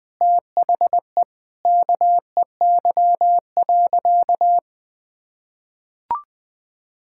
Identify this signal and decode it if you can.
Morse code, heard as a keyed tone, about 20 wpm, 700 Hz. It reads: THE KEY.
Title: Sentences with 3-Word Phrases - Spoken Beginning - 20wpm